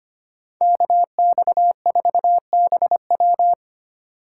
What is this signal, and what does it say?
Morse code, heard as a keyed tone, about 33 wpm, 700 Hz. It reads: KX4BW